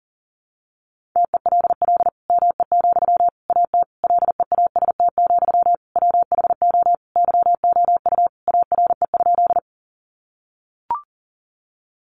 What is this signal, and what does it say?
Morse code, heard as a keyed tone, about 40 wpm, 700 Hz. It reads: TELL ME, AT LEAST, WHO YOU ARE?
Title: Sentences from Top 700 Words 40wpm